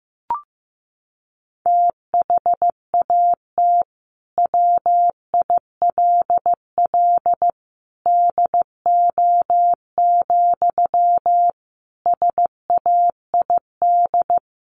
Morse code, heard as a keyed tone, about 15 wpm, 700 Hz. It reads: THAT WILL DO, SAID
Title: Sentences from Top 100 Words 15wpm